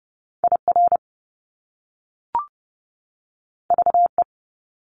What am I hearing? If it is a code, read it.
Morse code, heard as a keyed tone, about 30 wpm, 700 Hz. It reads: IR VE
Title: Top Two Letter Combinations in Words 30wpm